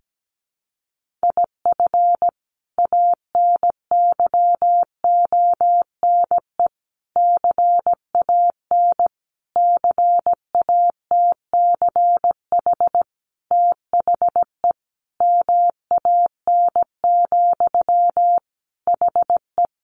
Morse code, heard as a keyed tone, about 17 wpm, 700 Hz. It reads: IF ANYONE CAN CATCH THE MAN, HE
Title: Sentences from Top 1500 Words 17wpm